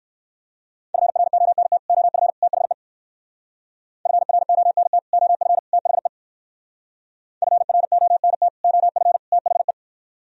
Morse code, heard as a keyed tone, about 56 wpm, 700 Hz. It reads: FRONT OF THE FRONT OF THE FRONT OF THE
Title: Speed-Racing - Sets of 3 Words 45wpm